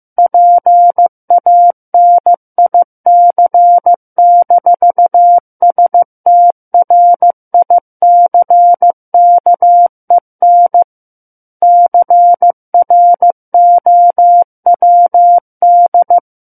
Morse code, heard as a keyed tone, about 15 wpm, 700 Hz. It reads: PANIC-STRICKEN CROWD
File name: Book One / 06-Chapter 6 - 15 WPM